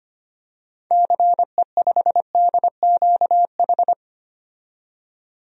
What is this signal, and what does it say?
Morse code, heard as a keyed tone, about 25 wpm, 700 Hz. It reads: CE5DQH
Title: DXCC-100 Call Signs - 6 Characters - 25wpm